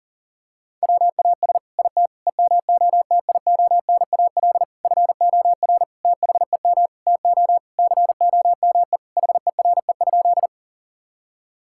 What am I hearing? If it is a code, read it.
Morse code, heard as a keyed tone, about 40 wpm, 690 Hz. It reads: WAS IT EMOTIONAL FOR THEM TO COME HERE?
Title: Sentences from Top 1000 Words - Review Part 5 - 40wpm